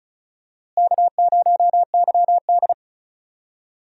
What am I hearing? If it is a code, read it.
Morse code, heard as a keyed tone, about 36 wpm, 700 Hz. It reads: K0YD
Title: USA Call Signs - 4 Characters - Repeated 3x - 35wpm